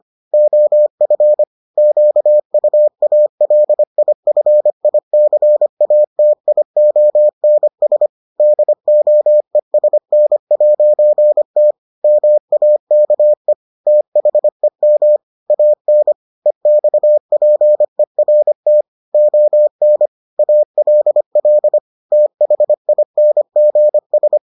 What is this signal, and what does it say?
Morse code, heard as a keyed tone, about 25 wpm, 600 Hz. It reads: OF QUALIFICATIONS DOESN'T MAKE THEM AN EXPERT ON ALL THINGS